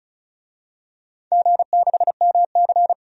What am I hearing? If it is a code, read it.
Morse code, heard as a keyed tone, about 34 wpm, 690 Hz. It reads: G6MC
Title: DXCC-100 Call Signs - 4 Characters - 35wpm